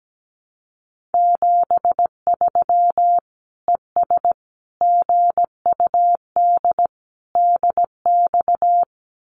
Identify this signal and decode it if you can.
Morse code, heard as a keyed tone, about 17 wpm, 700 Hz. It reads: 73 ES GUD DX
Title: Common QSO Phrases 17wpm